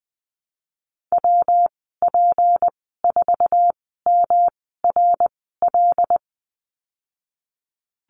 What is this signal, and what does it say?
Morse code, heard as a keyed tone, about 20 wpm, spaced as for 15 wpm, 700 Hz. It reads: WP4MRL
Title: 173 Lesson 28g - Number 9 - Call Signs 20wpm